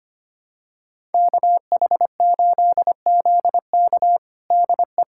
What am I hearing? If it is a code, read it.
Morse code, heard as a keyed tone, about 25 wpm, 700 Hz. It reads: KH8ZK DE
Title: USA Callsign DE Callsign - 5 Characters - 25wpm